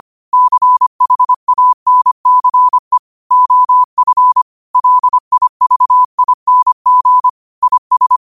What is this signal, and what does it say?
Morse code, heard as a keyed tone, about 25 wpm, 1000 Hz. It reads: CHANCE OF LIVING IS
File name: qotdfiles / 25wpm